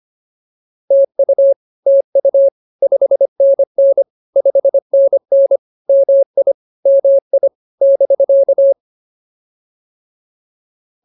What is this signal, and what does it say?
Morse code, heard as a keyed tone, about 24 wpm, 550 Hz. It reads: TU TU 5NN 5NN MI MI <BK>